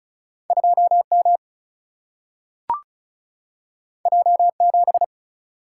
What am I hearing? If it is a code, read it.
Morse code, heard as a keyed tone, about 36 wpm, 700 Hz. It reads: 2M J7
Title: Single-Letter Single-Number 35wpm